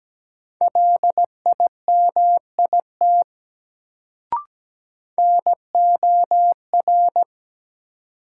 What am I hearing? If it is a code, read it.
Morse code, heard as a keyed tone, about 17 wpm, 700 Hz. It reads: LIMIT NOR
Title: Sentences from Top 1500 Words - New Word Drill Alt - 17wpm